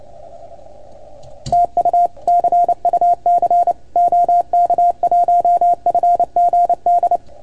Morse code, heard as a keyed tone, about 29 wpm, 690 Hz. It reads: TU CUC OK1FGD
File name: Archiv / CUC310 / OK1FGD CUC 310 pileup